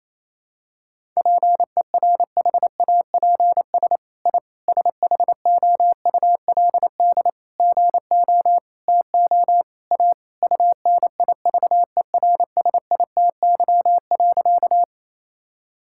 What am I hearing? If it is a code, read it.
Morse code, heard as a keyed tone, about 28 wpm, 700 Hz. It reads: PERHAPS I SHOULD GO TO A UNIVERSITY.
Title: Sentences from Top 1000 Words - Review Part 5 - 28wpm